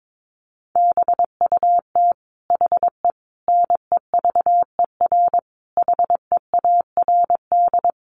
Morse code, heard as a keyed tone, about 22 wpm, 700 Hz. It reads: BUT HE NEVER HEARD
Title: Sentences from Top 400 Words 22wpm